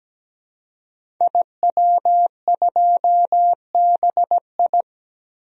Morse code, heard as a keyed tone, about 17 wpm, 700 Hz. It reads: IW2BI
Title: DXCC-100 Call Signs - 5 Characters - 17wpm